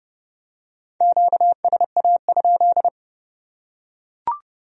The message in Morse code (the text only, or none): QSA?